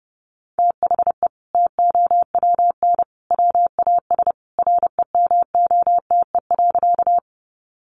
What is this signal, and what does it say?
Morse code, heard as a keyed tone, about 30 wpm, 700 Hz. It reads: THE TOWN WAS REMOTE.
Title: Sentences from Top 1300 Words 30wpm